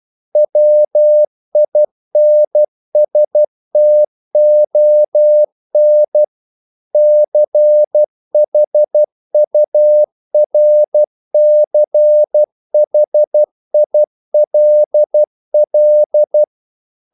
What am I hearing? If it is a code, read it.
Morse code, heard as a keyed tone, about 12 wpm, 600 Hz. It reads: WINSTON CHURCHILL
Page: English quotations in CW (telegraphy - morse)